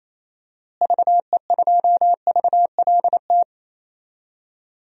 Morse code, heard as a keyed tone, about 28 wpm, 700 Hz. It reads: VE2VLT
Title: DXCC-100 Call Signs - 6 Characters - Repeated 2x - 28wpm